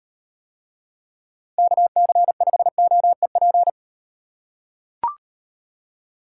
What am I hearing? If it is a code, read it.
Morse code, heard as a keyed tone, about 38 wpm, 700 Hz. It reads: KC5OEP